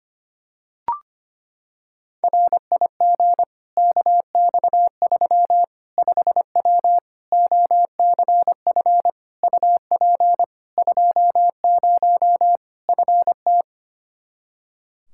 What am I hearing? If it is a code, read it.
Morse code, heard as a keyed tone, about 25 wpm, 700 Hz. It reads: RIG KX3 5W OCF UP 20 FT
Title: Practice QSO - Exchange 2 - Info - 25wpm